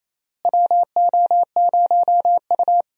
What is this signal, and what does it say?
Morse code, heard as a keyed tone, about 28 wpm, 700 Hz. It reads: WO0U
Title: USA Call Signs - 4 Characters - Repeated 3x - 28wpm